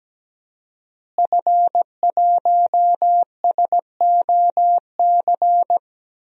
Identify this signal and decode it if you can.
Morse code, heard as a keyed tone, about 17 wpm, 700 Hz. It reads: F1SOC
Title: DXCC-100 Call Signs - 5 Characters - Repeated 2x - 17wpm